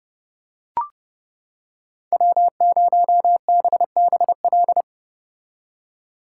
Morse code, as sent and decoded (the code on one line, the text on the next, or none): .-- ----- -... -... .-..
W0BBL